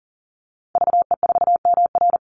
Morse code, heard as a keyed tone, about 40 wpm, 700 Hz. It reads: VE4MR
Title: Call Signs Repeated 3x - 40wpm